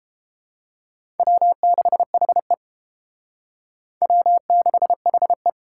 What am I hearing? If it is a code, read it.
Morse code, heard as a keyed tone, about 28 wpm, 700 Hz. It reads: W6HE W6HE